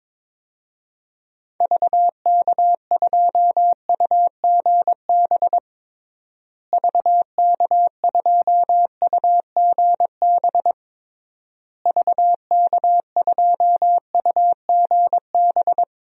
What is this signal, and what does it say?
Morse code, heard as a keyed tone, about 22 wpm, 700 Hz. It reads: VK2UGB VK2UGB VK2UGB
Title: DXCC-100 Call Signs - 6 Characters - Repeated 3x - 22wpm